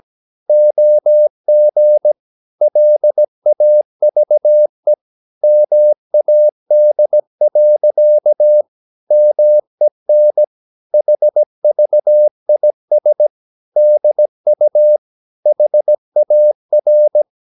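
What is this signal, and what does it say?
Morse code, heard as a keyed tone, about 17 wpm, 600 Hz. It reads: OG LAVE MAD. MEN HVIS DU HAR
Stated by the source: Slow (QRS) Morse practice at 17 wpm, from CW med Gnister.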